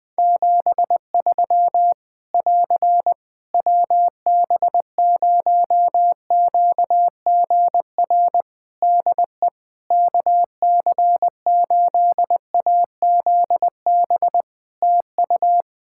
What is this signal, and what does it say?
Morse code, heard as a keyed tone, about 20 wpm, 700 Hz. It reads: 73 <AR> WB0QGR DE KC8AZB TU
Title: Practice QSO - Exchange 4 - Ending - 20wpm